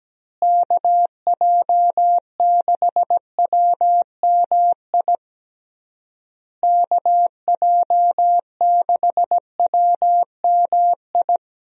Morse code, heard as a keyed tone, about 17 wpm, 700 Hz. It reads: KJ6WMI KJ6WMI
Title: USA Call Signs - 6 Characters - Repeated 3x - 17wpm